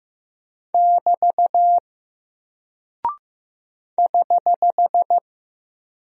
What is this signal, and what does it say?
Morse code, heard as a keyed tone, about 15 wpm, 700 Hz. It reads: <BT> <HH>
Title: Single Character (Letter:Number:Punctuation:Prosign) 15wpm